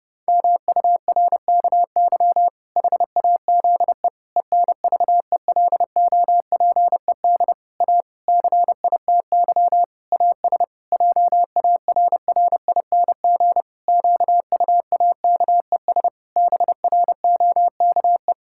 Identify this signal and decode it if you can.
Morse code, heard as a keyed tone, about 30 wpm, 700 Hz. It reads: MURKY HAZE ENVELOPED A CITY AS JARRING QUAKES BROKE
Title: Pangram Challenge v2 - 30wpm